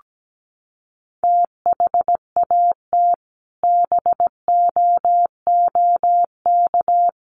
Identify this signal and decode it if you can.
Morse code, heard as a keyed tone, about 17 wpm, 700 Hz. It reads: THAT BOOK
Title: Sentences from Top 1500 Words 17wpm